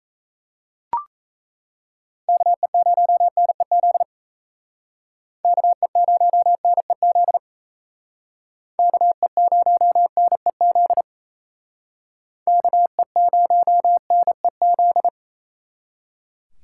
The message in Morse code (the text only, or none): KE0NEZ KE0NEZ KE0NEZ KE0NEZ